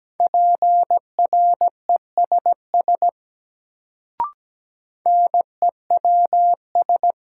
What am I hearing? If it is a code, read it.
Morse code, heard as a keyed tone, about 17 wpm, 700 Hz. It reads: PRESS NEWS
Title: Sentences from Top 800 Words - New Word Drill - 17wpm